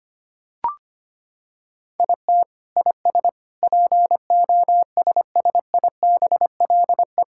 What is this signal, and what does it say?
Morse code, heard as a keyed tone, about 25 wpm, 700 Hz. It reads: IT IS POSSIBLE
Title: Sentences from Top 1600 Words 25wpm